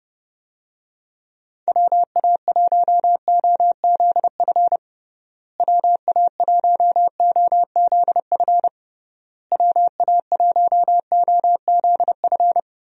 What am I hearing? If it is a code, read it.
Morse code, heard as a keyed tone, about 32 wpm, 700 Hz. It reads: WA1OZF WA1OZF WA1OZF
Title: USA Call Signs - 6 Characters - Repeated 3x - 30wpm